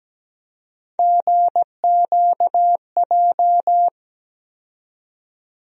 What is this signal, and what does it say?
Morse code, heard as a keyed tone, about 17 wpm, 700 Hz. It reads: GQJ
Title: USA Call Sign Postfixes - 3 Letters 17wpm